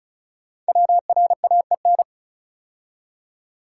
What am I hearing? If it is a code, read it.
Morse code, heard as a keyed tone, about 36 wpm, 690 Hz. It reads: WRAEN
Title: Code Groups - 5 Letters - 35wpm